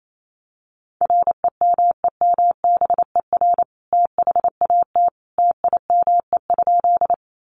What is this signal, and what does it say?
Morse code, heard as a keyed tone, about 28 wpm, 700 Hz. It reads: REMEMBER THAT TIME?
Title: Sentences from Top 400 Words 28wpm